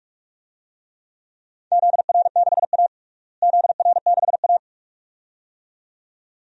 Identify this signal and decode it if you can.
Morse code, heard as a keyed tone, about 41 wpm, 700 Hz. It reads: ZR6A ZR6A
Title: DXCC-100 Call Signs - 4 Characters - Repeated 2x - 45wpm